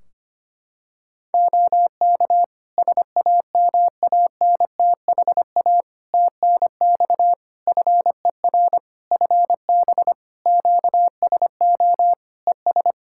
Morse code, heard as a keyed tone, about 25 wpm, 700 Hz. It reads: OK SAMANTHA TNX FER FB QSO ES
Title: Practice QSO - Exchange 4 - Ending - 25wpm